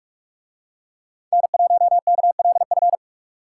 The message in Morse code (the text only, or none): N1KLF